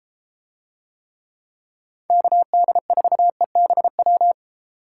Morse code, heard as a keyed tone, about 34 wpm, 700 Hz. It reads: KD4EBW